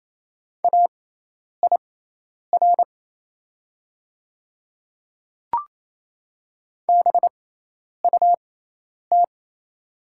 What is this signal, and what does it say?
Morse code, heard as a keyed tone, about 27 wpm, spaced as for 10 wpm, 700 Hz. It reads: AIR BUT
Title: 6x Character Spacing - Top 300 Words - 28wpm